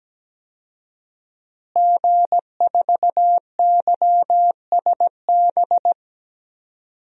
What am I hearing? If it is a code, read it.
Morse code, heard as a keyed tone, about 17 wpm, 700 Hz. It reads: G4YSB
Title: DXCC-100 Call Signs - 5 Characters - Repeated 3x - 17wpm